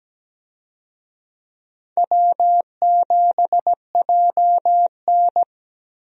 Morse code, heard as a keyed tone, about 17 wpm, 700 Hz. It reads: W7JN
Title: USA Call Signs - 4 Characters - Repeated 3x - 17wpm